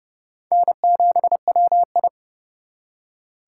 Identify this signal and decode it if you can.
Morse code, heard as a keyed tone, about 37 wpm, 700 Hz. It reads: N7WI
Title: Speed Racing - USA 1x2 Call Signs 30wpm